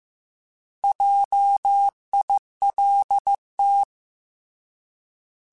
Morse code, heard as a keyed tone, about 15 wpm, 800 Hz. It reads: JILT
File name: Alphabetical / Alphabetical mp3s / Js @13